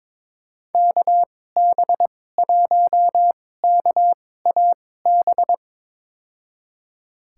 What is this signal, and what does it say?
Morse code, heard as a keyed tone, about 21 wpm, spaced as for 16 wpm, 700 Hz. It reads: KB1KAB
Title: USA Call Signs - 6 Characters - 2x Character Spacing - 22wpm